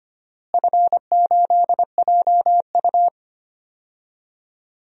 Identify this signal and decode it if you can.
Morse code, heard as a keyed tone, about 25 wpm, 700 Hz. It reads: F8JU
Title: DXCC-100 Call Signs - 4 Characters - Repeated 2x - 25wpm